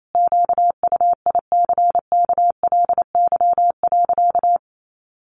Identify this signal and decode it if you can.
Morse code, heard as a keyed tone, about 28 wpm, 700 Hz. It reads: QUICKLY.
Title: Sentences from Top 700 Words 28wpm